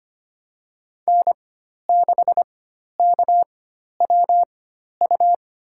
Morse code, heard as a keyed tone, about 25 wpm, spaced as for 12 wpm, 700 Hz. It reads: N6KWU
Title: USA Call Signs - 5 Characters - 4x Character Spacing - 25wpm